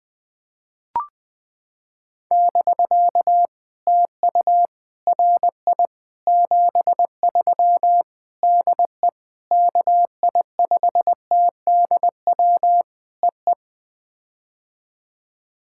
Morse code, heard as a keyed tone, about 20 wpm, 700 Hz. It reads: <BK> TU RI 73 DE KI5TDW EE
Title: POTA QSO Practice - 20wpm